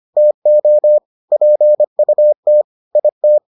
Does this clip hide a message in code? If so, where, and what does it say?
Yes: Morse code from 0.2 to 3.4 s, heard as a keyed tone, about 25 wpm, 600 Hz. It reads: TO PUT IT